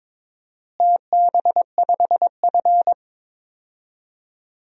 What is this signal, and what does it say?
Morse code, heard as a keyed tone, about 22 wpm, 700 Hz. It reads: TB5F
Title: DXCC-100 Call Signs - 4 Characters - Repeated 3x - 22wpm